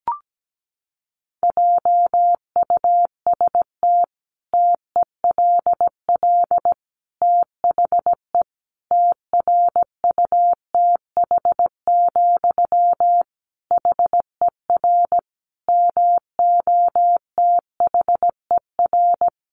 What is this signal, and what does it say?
Morse code, heard as a keyed tone, about 17 wpm, 700 Hz. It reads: JUST TELL THE TRUTH, HER MOTHER
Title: Sentences from Top 800 Words 17wpm